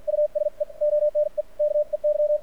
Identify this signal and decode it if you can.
Morse code, heard as a keyed tone, about 56 wpm, spaced as for 47 wpm, 600 Hz. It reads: USE ONE KEY
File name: archived / CRSnet